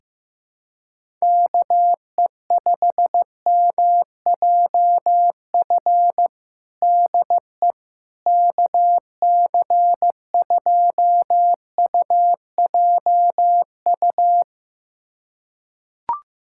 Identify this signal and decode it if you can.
Morse code, heard as a keyed tone, about 15 wpm, 700 Hz. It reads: KE5MJF DE KC2UJU
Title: USA Callsign DE Callsign - 6 Characters - 15wpm